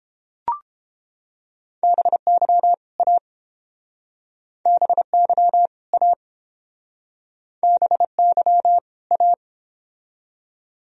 Morse code, heard as a keyed tone, about 31 wpm, 700 Hz. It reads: BY A BY A BY A